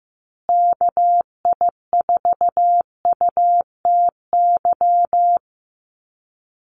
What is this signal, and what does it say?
Morse code, heard as a keyed tone, about 15 wpm, 700 Hz. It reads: KI4UTY
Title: USA Call Signs Repeated 2x 15wpm